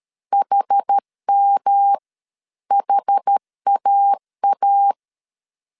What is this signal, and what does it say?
Morse code, heard as a keyed tone, about 13 wpm, spaced as for 11 wpm, 800 Hz. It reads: HMHAA
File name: CW / DET / 1 3 1500C S B 12wpm 2828